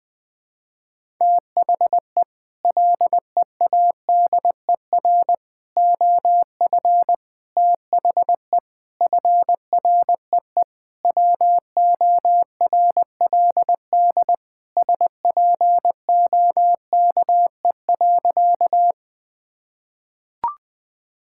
Morse code, heard as a keyed tone, about 21 wpm, 700 Hz. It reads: THE LEADER OF THE FREE WORLD SPOKE.
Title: Sentences with 6-Word Phrases - 20wpm